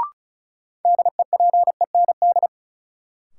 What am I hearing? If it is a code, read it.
Morse code, heard as a keyed tone, about 36 wpm, 700 Hz. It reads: DEPEND
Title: Sentences from Top 1300 Words - New Word Drill Alt - 35wpm